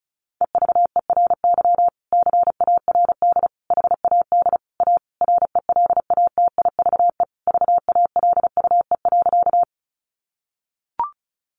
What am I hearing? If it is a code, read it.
Morse code, heard as a keyed tone, about 34 wpm, 700 Hz. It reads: EVERY CARD HAD A RELATIVE VALUE.
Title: Sentences from Top 1400 Words 35wpm